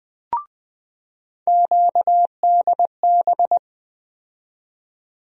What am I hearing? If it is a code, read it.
Morse code, heard as a keyed tone, about 20 wpm, 700 Hz. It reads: QDB
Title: USA Call Sign Postfixes - 3 Letters 20wpm